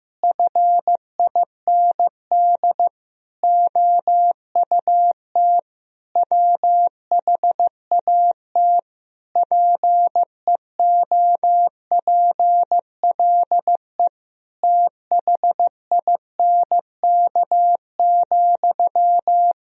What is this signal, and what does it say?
Morse code, heard as a keyed tone, about 15 wpm, 700 Hz. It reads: FIND OUT WHAT PEOPLE THINK,
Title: Sentences from Top 1000 Words - Review Part 4 - 15wpm